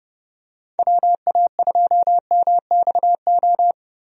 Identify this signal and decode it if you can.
Morse code, heard as a keyed tone, about 32 wpm, 700 Hz. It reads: WA2MXO
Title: USA Call Signs - 6 Characters - Repeated 2x - 30wpm